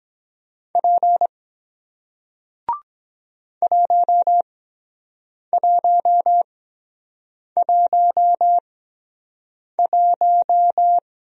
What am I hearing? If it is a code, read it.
Morse code, heard as a keyed tone, about 23 wpm, 700 Hz. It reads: P 1 1 1 1